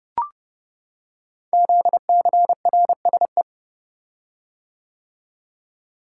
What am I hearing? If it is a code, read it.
Morse code, heard as a keyed tone, about 30 wpm, 700 Hz. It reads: ZCRSE
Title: Code Groups - 5 Letters - 30wpm